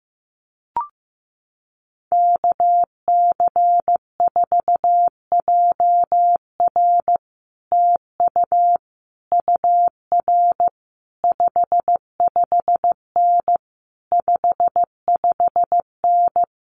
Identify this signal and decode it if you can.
Morse code, heard as a keyed tone, about 16 wpm, 700 Hz. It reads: KC4JR TU UR 55N 55N
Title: Common QSO Phrases Encore 15wpm